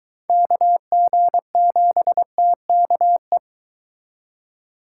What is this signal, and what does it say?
Morse code, heard as a keyed tone, about 22 wpm, 700 Hz. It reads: KG7TKE